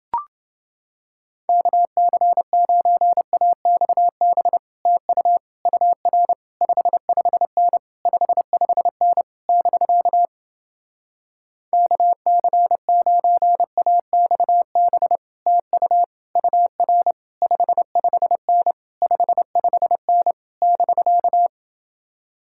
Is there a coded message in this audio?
Yes: Morse code from 1.5 to 21.5 s, heard as a keyed tone, about 29 wpm, 700 Hz. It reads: KC9AXB TU UR 55N 55N <BK> KC9AXB TU UR 55N 55N <BK>